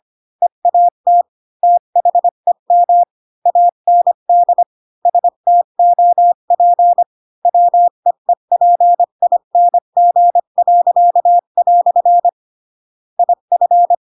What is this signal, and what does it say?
Morse code, heard as a keyed tone, about 25 wpm, 700 Hz. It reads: EAT THEM AND STOP WEEPING." IF
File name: Pinocchio / Pinocchio ch7 20 25 ik1hge